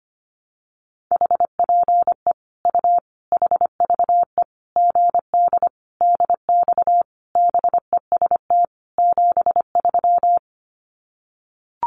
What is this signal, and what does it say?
Morse code, heard as a keyed tone, about 25 wpm, 700 Hz. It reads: HPE U HVE GD DX BEST 73
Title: Common QSO Phrases 25wpm